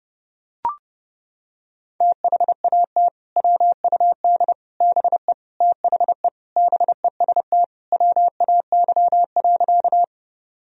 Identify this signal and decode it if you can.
Morse code, heard as a keyed tone, about 30 wpm, 700 Hz. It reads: THAT WUD BE THE BEST WAY.
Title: Sentences from Top 400 Words and Abbreviations 30wpm